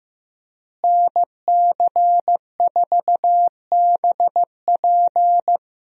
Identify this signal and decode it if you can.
Morse code, heard as a keyed tone, about 15 wpm, 700 Hz. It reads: NC4BP
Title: USA Call Signs - 5 Characters - Repeated 2x - 15wpm